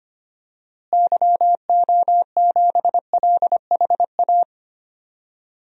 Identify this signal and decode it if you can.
Morse code, heard as a keyed tone, about 25 wpm, 700 Hz. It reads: YO7LHA
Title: DXCC-100 Call Signs - 6 Characters - Repeated 3x - 25wpm